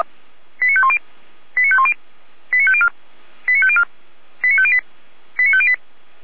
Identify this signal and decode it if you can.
Five-tone selective call (ZVEI), heard as quick runs of five tones, tones of about 70 ms. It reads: 89510 89510 89504 89504 89508 89508